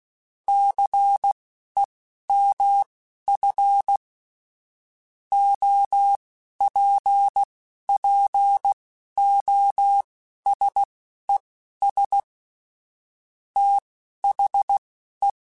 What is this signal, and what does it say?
Morse code, heard as a keyed tone, about 16 wpm, spaced as for 11 wpm, 800 Hz. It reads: CEMF OPPOSES THE
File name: Technical Subjects / Technical Subjects-mp3s / XL @10